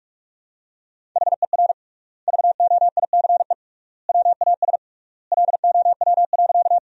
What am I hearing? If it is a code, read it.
Morse code, heard as a keyed tone, about 47 wpm, 700 Hz. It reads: HER VOICE WAS LOW.
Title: 3x Word-Spacing - Sentences from Top 500 Words - 45wpm